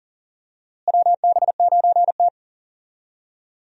Morse code, heard as a keyed tone, about 40 wpm, 700 Hz. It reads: WB9T
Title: USA Call Signs - 4 Characters - Repeated 2x - 40wpm